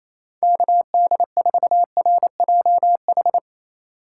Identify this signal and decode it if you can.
Morse code, heard as a keyed tone, about 35 wpm, 700 Hz. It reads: KD4RJH
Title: Speed Racing - USA 2x3 Call Signs 28wpm